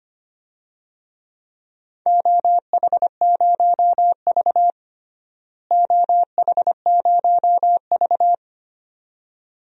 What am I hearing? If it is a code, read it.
Morse code, heard as a keyed tone, about 25 wpm, 700 Hz. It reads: OH0V OH0V
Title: DXCC-100 Call Signs - 4 Characters - Repeated 2x - 25wpm